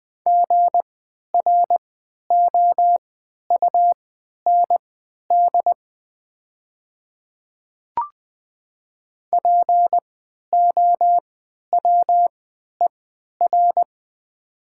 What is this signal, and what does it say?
Morse code, heard as a keyed tone, about 20 wpm, spaced as for 11 wpm, 700 Hz. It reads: GROUND POWER
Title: 3x Character Spacing - Top 500 Words - 20wpm